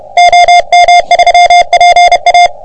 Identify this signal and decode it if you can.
Morse code, heard as a keyed tone, about 30 wpm, 720 Hz. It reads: OM3PA
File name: OM3PA PILUP CUC 388